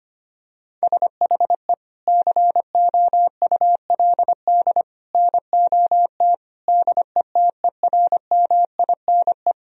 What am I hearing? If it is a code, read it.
Morse code, heard as a keyed tone, about 25 wpm, 700 Hz. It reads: SHE COULD NOT DETERMINE
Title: Sentences from Top 1000 Words - Review Part 4 - 25wpm